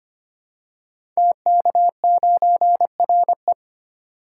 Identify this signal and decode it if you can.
Morse code, heard as a keyed tone, about 25 wpm, 700 Hz. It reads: TK9RE
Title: Code Groups - 5 Letter:Numbers - 25wpm